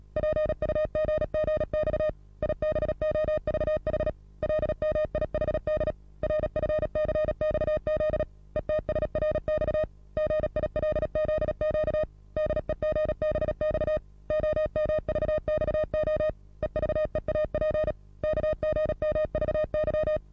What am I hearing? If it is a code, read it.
Morse code, heard as a keyed tone, about 37 wpm, 620 Hz. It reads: PUGGX IBOVH LMIHD RFCXZ ETSRX GILZQ DEGBX OMVXO EVEAP KGMVY
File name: repos / Morsekurs / DJ4UF Morsen27